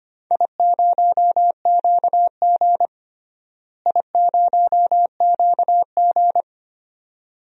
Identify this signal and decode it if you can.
Morse code, heard as a keyed tone, about 25 wpm, 700 Hz. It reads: I0QG I0QG
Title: DXCC-100 Call Signs - 4 Characters - Repeated 2x - 25wpm